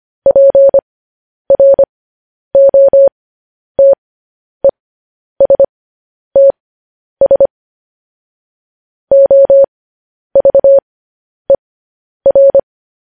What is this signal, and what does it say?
Morse code, heard as a keyed tone, about 25 wpm, spaced as for 10 wpm, 550 Hz. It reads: PROTESTS OVER
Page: news-headlines-in-morse-code-at-10-wpm